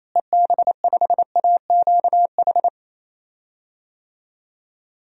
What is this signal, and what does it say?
Morse code, heard as a keyed tone, about 28 wpm, 700 Hz. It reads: EB5AQH
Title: DXCC-100 Call Signs - 6 Characters - Repeated 3x - 28wpm